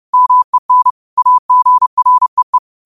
Morse code, heard as a keyed tone, about 30 wpm, 1000 Hz. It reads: MEN AGREE
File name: qotdfiles / 30wpm